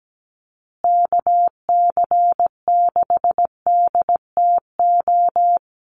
Morse code, heard as a keyed tone, about 17 wpm, 700 Hz. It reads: KC6DTO
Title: USA Call Signs Repeated 3x 17wpm